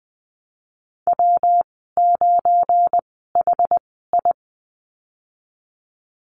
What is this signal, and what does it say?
Morse code, heard as a keyed tone, about 20 wpm, spaced as for 15 wpm, 700 Hz. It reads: W9HI
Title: 179 Lesson 29g - Slash - Call Signs 20wpm